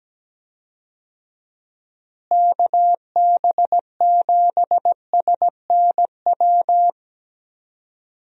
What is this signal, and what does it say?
Morse code, heard as a keyed tone, about 17 wpm, 700 Hz. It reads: KB7SNW
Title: USA Call Signs - 6 Characters - Repeated 3x - 17wpm